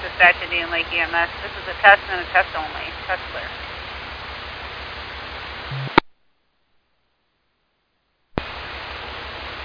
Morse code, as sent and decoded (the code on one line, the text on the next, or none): none